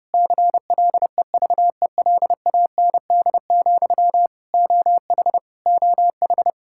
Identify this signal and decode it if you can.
Morse code, heard as a keyed tone, about 32 wpm, 700 Hz. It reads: CLEVELAND, OH OH
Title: Practice QSO - Exchange 1 - Beginning - 30wpm